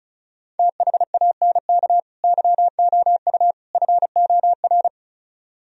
Morse code, heard as a keyed tone, about 44 wpm, 700 Hz. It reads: THANK YOU FOR